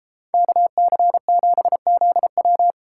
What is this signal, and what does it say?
Morse code, heard as a keyed tone, about 33 wpm, 700 Hz. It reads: KC7ZW